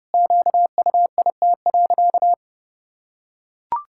Morse code, heard as a keyed tone, about 30 wpm, 700 Hz. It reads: QUIT.
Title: Pangram Challenge v2 - 30wpm